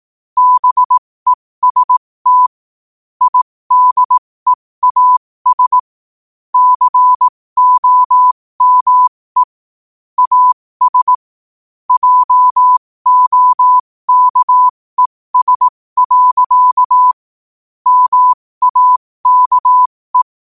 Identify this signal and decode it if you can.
Morse code, heard as a keyed tone, about 18 wpm, spaced as for 15 wpm, 1000 Hz. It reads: BEST IDEAS COME AS JOKES. MAKE
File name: qotdfiles / 15wpm